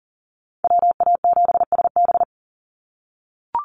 Morse code, heard as a keyed tone, about 40 wpm, 700 Hz. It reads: WA7SB
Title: NA SOTA Chaser Callsigns 40wpm